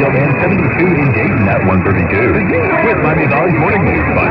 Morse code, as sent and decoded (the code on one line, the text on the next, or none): none